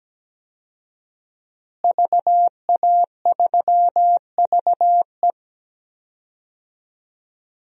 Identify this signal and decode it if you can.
Morse code, heard as a keyed tone, about 17 wpm, 700 Hz. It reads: VA3VE
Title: DXCC-100 Call Signs - 5 Characters - Repeated 3x - 17wpm